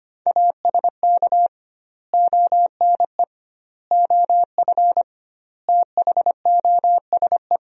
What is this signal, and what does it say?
Morse code, heard as a keyed tone, about 25 wpm, 700 Hz. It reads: ASK ONE OF THOSE
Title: 2x Word-Spacing - Sentences from Top 300 Words - 25wpm